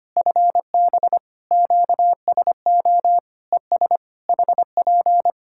Morse code, heard as a keyed tone, about 25 wpm, 700 Hz. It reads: FB QSO ES HP
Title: K1USN SST QSO Practice 25wpm